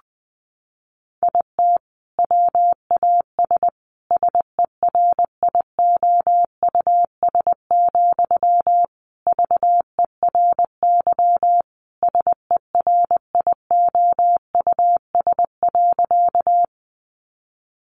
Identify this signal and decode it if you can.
Morse code, heard as a keyed tone, about 19 wpm, 700 Hz. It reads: IT WAS SERIOUS, VERY SERIOUS.
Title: Sentences from Top 1100 Words 20wpm